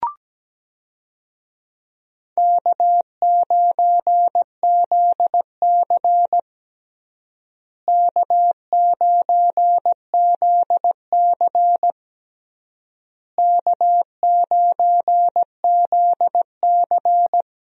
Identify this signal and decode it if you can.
Morse code, heard as a keyed tone, about 17 wpm, 700 Hz. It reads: K9ZC K9ZC K9ZC
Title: USA Call Signs - 4 Characters - Repeated 3x - 17wpm